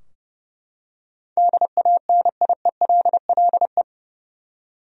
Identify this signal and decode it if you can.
Morse code, heard as a keyed tone, about 30 wpm, 700 Hz. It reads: DANIELLE
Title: USA Common Names - 30wpm